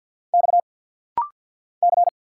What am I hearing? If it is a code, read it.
Morse code, heard as a keyed tone, about 49 wpm, 700 Hz. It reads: X C
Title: Single Letter-Number Alt - Rapid-Fire 50wpm